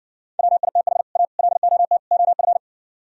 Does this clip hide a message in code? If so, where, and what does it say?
Yes: Morse code from 0.4 to 2.6 s, heard as a keyed tone, about 68 wpm, 700 Hz. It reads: WITH A LOT OF